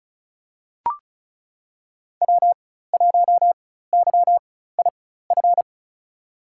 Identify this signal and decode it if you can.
Morse code, heard as a keyed tone, about 34 wpm, spaced as for 16 wpm, 700 Hz. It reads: W1YIF
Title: USA Call Signs - 5 Characters - 4x Character Spacing - 35wpm